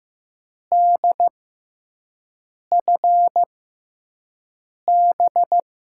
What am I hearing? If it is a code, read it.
Morse code, heard as a keyed tone, about 15 wpm, spaced as for 5 wpm, 700 Hz. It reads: DFB
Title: USA Call Signs - 6 Characters - 6x Character Spacing - 15wpm